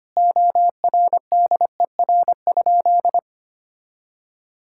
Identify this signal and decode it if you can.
Morse code, heard as a keyed tone, about 25 wpm, 700 Hz. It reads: ORDER?
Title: Sentences from Top 1000 Words - Review Part 6 - 25wpm